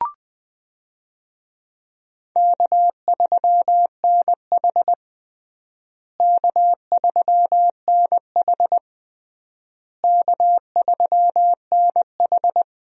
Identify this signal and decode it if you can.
Morse code, heard as a keyed tone, about 20 wpm, 700 Hz. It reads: K3NH K3NH K3NH
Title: USA Call Signs - 4 Characters - Repeated 3x - 20wpm